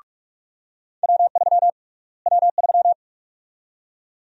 Morse code, heard as a keyed tone, about 45 wpm, 700 Hz. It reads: W3 W3
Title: USA Call Sign Prefixes - Repeated 2x - 45wpm